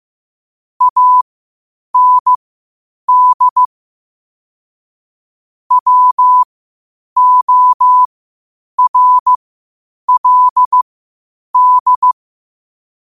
Morse code, heard as a keyed tone, about 15 wpm, spaced as for 8 wpm, 1000 Hz. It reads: AND WORLD
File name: qotdfiles / 8wpm